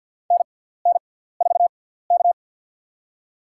Morse code, heard as a keyed tone, about 52 wpm, spaced as for 17 wpm, 700 Hz. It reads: NN4K
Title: USA Call Signs - 4 Characters - 6x Character Spacing - 50wpm